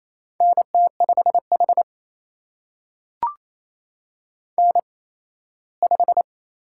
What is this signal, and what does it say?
Morse code, heard as a keyed tone, about 28 wpm, spaced as for 17 wpm, 700 Hz. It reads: NT5H N5
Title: USA Call Signs - 4 Characters - 8x Character Spacing - 28wpm